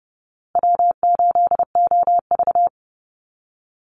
Morse code, heard as a keyed tone, about 30 wpm, 700 Hz. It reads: W8OV
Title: Easy Call Signs Repeated 3x - 30wpm